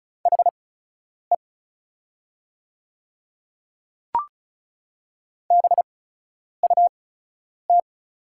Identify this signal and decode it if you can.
Morse code, heard as a keyed tone, about 36 wpm, spaced as for 10 wpm, 700 Hz. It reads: HE BUT
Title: 8x Character Spacing - Top 200 Words - 35wpm